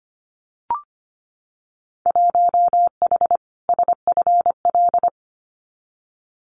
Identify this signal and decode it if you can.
Morse code, heard as a keyed tone, about 25 wpm, 700 Hz. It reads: 1H SFL
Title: Winter Field Day Exchanges 25wpm